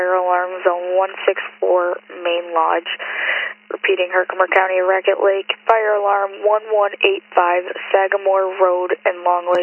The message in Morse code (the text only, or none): none